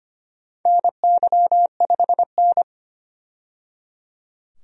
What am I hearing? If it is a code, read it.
Morse code, heard as a keyed tone, about 25 wpm, 700 Hz. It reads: NY5N
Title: USA Call Signs - 4 Characters - 25wpm